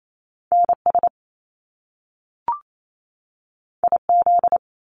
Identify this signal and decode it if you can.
Morse code, heard as a keyed tone, about 28 wpm, 700 Hz. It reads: NS IZ
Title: Top Two Letter Combinations in Words 28wpm